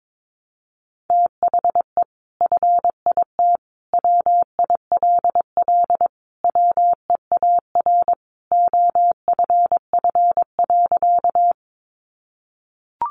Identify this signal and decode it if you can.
Morse code, heard as a keyed tone, about 22 wpm, 700 Hz. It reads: THE FIT WILL WEAR OFF.
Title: Sentences from Top 1500 Words 22wpm